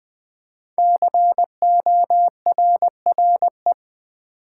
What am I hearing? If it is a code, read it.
Morse code, heard as a keyed tone, about 20 wpm, 700 Hz. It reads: CORRE
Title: Build A Word - 5 Letters At A Time - 10-Letter Words - 20wpm